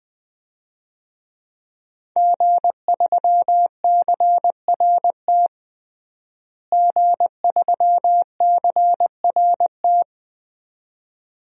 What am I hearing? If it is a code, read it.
Morse code, heard as a keyed tone, about 20 wpm, 700 Hz. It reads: G3CRT G3CRT
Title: DXCC-100 Call Signs - 5 Characters - Repeated 2x - 20wpm